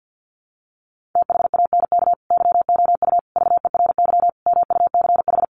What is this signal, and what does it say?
Morse code, heard as a keyed tone, about 50 wpm, 700 Hz. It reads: THANK YOU VERY MUCH
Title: Sets of 5 Words 50wpm